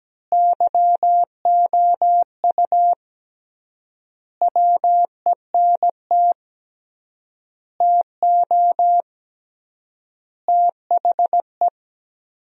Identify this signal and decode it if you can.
Morse code, heard as a keyed tone, about 17 wpm, 700 Hz. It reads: YOU WENT TO THE
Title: 3x Word-Spacing - Sentences from Top 400 Words - 17wpm